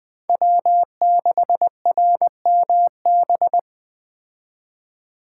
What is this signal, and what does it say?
Morse code, heard as a keyed tone, about 19 wpm, 700 Hz. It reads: W6RMB